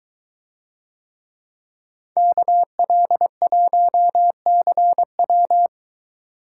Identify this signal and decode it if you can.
Morse code, heard as a keyed tone, about 22 wpm, 700 Hz. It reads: KL1CW